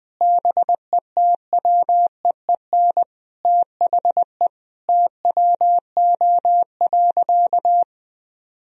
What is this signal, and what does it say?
Morse code, heard as a keyed tone, about 20 wpm, 700 Hz. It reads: BETWEEN THE TWO.
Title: Sentences from Top 1000 Words - Review Part 4 - 20wpm